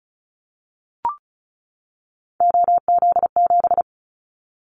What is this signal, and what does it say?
Morse code, heard as a keyed tone, about 36 wpm, 690 Hz. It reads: OZ7
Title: Callsign Prefixes for Top 50 DXCC - 35wpm